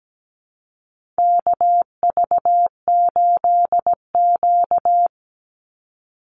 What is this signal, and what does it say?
Morse code, heard as a keyed tone, about 17 wpm, 700 Hz. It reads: KV8Q
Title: Call Signs - 17wpm